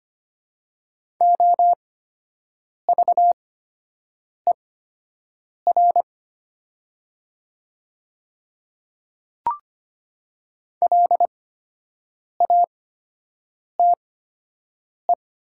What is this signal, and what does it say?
Morse code, heard as a keyed tone, about 25 wpm, spaced as for 7 wpm, 700 Hz. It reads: OVER LATE
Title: 8x Character Spacing - Top 300 Words - 25wpm